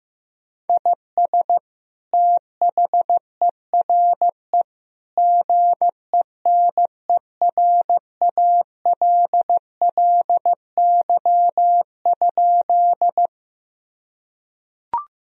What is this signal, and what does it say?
Morse code, heard as a keyed tone, about 15 wpm, 700 Hz. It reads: IS THERE GENERALLY?
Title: Sentences from Top 1000 Words - Review Part 4 - 15wpm